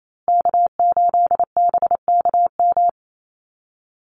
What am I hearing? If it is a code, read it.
Morse code, heard as a keyed tone, about 28 wpm, 700 Hz. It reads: K8BKM
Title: Call Signs Repeated 2x - 28wpm